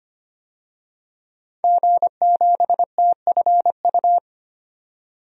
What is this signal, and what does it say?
Morse code, heard as a keyed tone, about 25 wpm, 700 Hz. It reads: G7TFU
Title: DXCC-100 Call Signs - 5 Characters - 25wpm